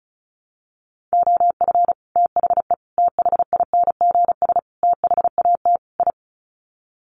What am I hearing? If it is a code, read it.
Morse code, heard as a keyed tone, about 34 wpm, 700 Hz. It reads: OF THE THINGS THAT I
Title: Sets of 5 Words Encore 35wpm